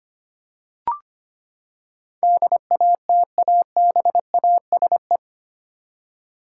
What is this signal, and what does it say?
Morse code, heard as a keyed tone, about 25 wpm, 700 Hz. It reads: DATABASE
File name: Eight Letter Words - 25wpm